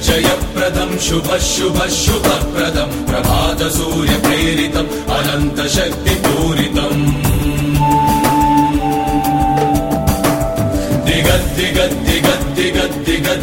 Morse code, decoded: none